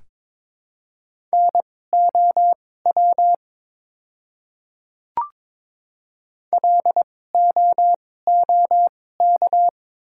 Morse code, heard as a keyed tone, about 22 wpm, spaced as for 16 wpm, 690 Hz. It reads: NOW LOOK
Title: 2x Character Spacing - Top 100 Words - 22wpm